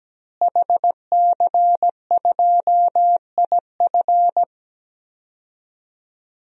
Morse code, heard as a keyed tone, about 17 wpm, 700 Hz. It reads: HC2IF
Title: DXCC-100 Call Signs - 5 Characters - Repeated 2x - 17wpm